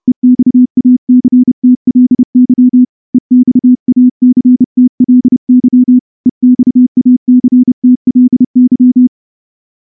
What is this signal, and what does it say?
Morse code, heard as a keyed tone, about 28 wpm, 250 Hz. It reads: EXACTLY EXACTLY EXACTLY
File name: morse-training-30WPM